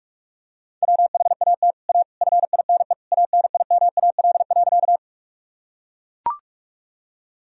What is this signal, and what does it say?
Morse code, heard as a keyed tone, about 43 wpm, 700 Hz. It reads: WHAT A FINE ANIMAL.
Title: Sentences from Top 1000 Words - Review Part 1 - 45wpm